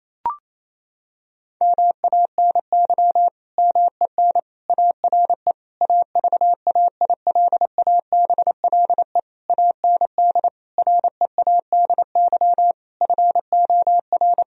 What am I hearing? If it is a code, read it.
Morse code, heard as a keyed tone, about 28 wpm, 700 Hz. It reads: MANY MEN ARE AVAILABLE AND READY FOR
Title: Sentences from Top 1000 Words - Review Part 3 - 28wpm